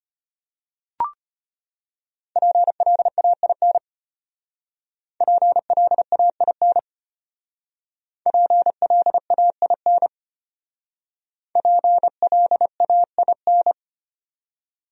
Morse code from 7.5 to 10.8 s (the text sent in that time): PLAIN